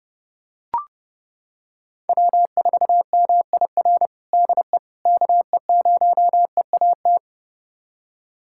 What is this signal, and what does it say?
Morse code, heard as a keyed tone, about 32 wpm, 700 Hz. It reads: W4MIR DE KE0EAT
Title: USA Callsign DE Callsign 30wpm